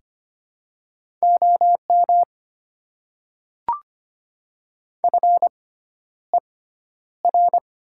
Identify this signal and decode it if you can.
Morse code, heard as a keyed tone, about 25 wpm, spaced as for 11 wpm, 690 Hz. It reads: OM FER
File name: Top 90 QSO Elements - 6x Character Spacing - 25wpm